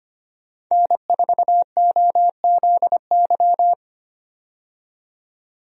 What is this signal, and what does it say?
Morse code, heard as a keyed tone, about 25 wpm, 700 Hz. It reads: N4OZY
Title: USA Call Signs - 5 Characters - 25wpm